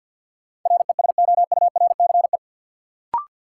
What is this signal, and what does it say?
Morse code, heard as a keyed tone, about 53 wpm, 700 Hz. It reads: RESOURCE
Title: Sentences from Top 1300 Words - New Word Drill - 50wpm